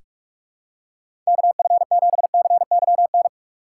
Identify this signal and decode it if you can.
Morse code, heard as a keyed tone, about 46 wpm, 700 Hz. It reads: KF7CYN
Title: USA Call Signs - 6 Characters - 2x Character Spacing - 45wpm